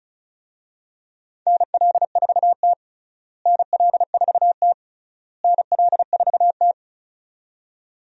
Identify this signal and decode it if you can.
Morse code, heard as a keyed tone, about 36 wpm, 700 Hz. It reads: NL4T NL4T NL4T
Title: USA Call Signs - 4 Characters - Repeated 3x - 35wpm